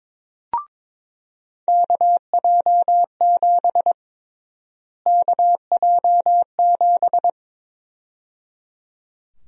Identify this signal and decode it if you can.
Morse code, heard as a keyed tone, about 22 wpm, 700 Hz. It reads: KJ7 KJ7
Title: USA Call Sign Prefixes - Repeated 2x - 22wpm